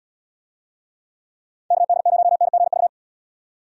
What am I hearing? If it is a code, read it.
Morse code, heard as a keyed tone, about 72 wpm, 700 Hz. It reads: KD0NGV